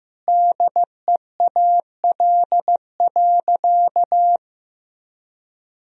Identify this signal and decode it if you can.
Morse code, heard as a keyed tone, about 16 wpm, 700 Hz. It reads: DEAL.
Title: Sentences from Top 1000 Words - Review Part 4 - 15wpm